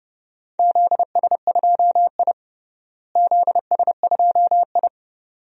Morse code, heard as a keyed tone, about 30 wpm, 700 Hz. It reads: ZS2I ZS2I
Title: DXCC-100 Call Signs - 4 Characters - Repeated 2x - 30wpm